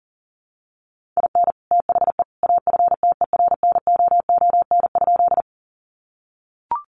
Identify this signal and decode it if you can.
Morse code, heard as a keyed tone, about 40 wpm, 700 Hz. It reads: IN THE AFTERNOON?
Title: Sentences from Top 1500 Words 40wpm